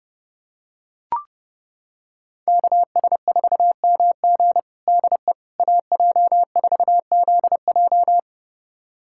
Morse code, heard as a keyed tone, about 32 wpm, 700 Hz. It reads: KS4MG DE AJ4ZJ
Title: USA Callsign DE Callsign - 5 Characters - 30wpm